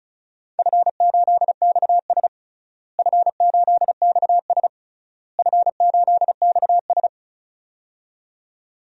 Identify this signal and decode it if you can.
Morse code, heard as a keyed tone, about 34 wpm, 700 Hz. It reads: F8XS F8XS F8XS
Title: DXCC-100 Call Signs - 4 Characters - Repeated 3x - 35wpm